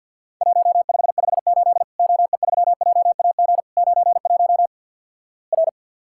Morse code, heard as a keyed tone, about 50 wpm, 700 Hz. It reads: 1558 OE3JAG 91 R
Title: Worked All Europe DX Contest - QTC Exchange with Repeats - 50wpm